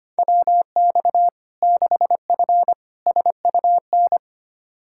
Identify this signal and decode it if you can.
Morse code, heard as a keyed tone, about 25 wpm, 700 Hz. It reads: WX 6F SUN
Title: Practice QSO - Exchange 2 - Info - 25wpm